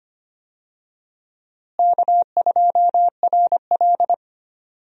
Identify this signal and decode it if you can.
Morse code, heard as a keyed tone, about 25 wpm, 700 Hz. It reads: K2RL
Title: USA Call Signs - 4 Characters - Repeated 2x - 25wpm